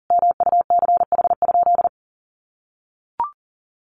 Morse code, heard as a keyed tone, about 40 wpm, 700 Hz. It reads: MUCH?
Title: Sentences from Top 400 Words 40wpm